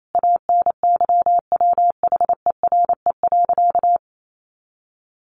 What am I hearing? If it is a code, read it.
Morse code, heard as a keyed tone, about 28 wpm, 700 Hz. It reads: ANYWHERE.
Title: Sentences from Top 1400 Words 28wpm